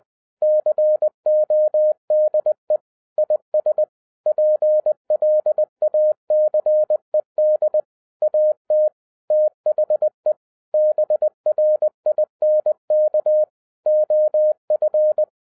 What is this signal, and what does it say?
Morse code, heard as a keyed tone, about 20 wpm, 610 Hz. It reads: CODE IS PLACED AT THE BRINK OF